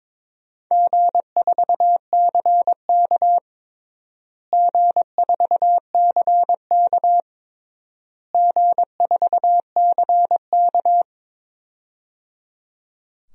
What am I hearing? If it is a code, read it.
Morse code, heard as a keyed tone, about 22 wpm, 700 Hz. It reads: G4CK G4CK G4CK
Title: DXCC-100 Call Signs - 4 Characters - Repeated 3x - 22wpm